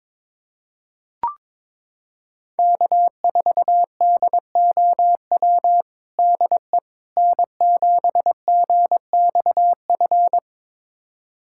Morse code, heard as a keyed tone, about 22 wpm, 700 Hz. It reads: K4DOW DE N7GXF
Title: USA Callsign DE Callsign - 5 Characters - 22wpm